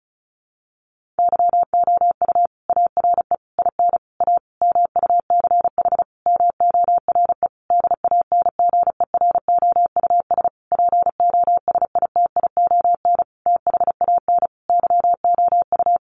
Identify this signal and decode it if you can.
Morse code, heard as a keyed tone, about 34 wpm, 700 Hz. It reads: YOU ARE IN A MUCH MORE DANGEROUS POSITION THAN YOU
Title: Sentences from Top 1400 Words 35wpm